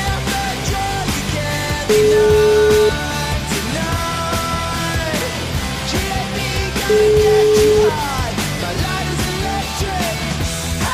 Ringback tone (Europe, 425 Hz), with noise.